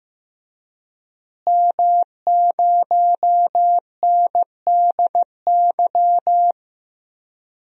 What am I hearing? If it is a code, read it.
Morse code, heard as a keyed tone, about 15 wpm, 700 Hz. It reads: M0NDY
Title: DXCC-100 Call Signs - 5 Characters - Repeated 3x - 15wpm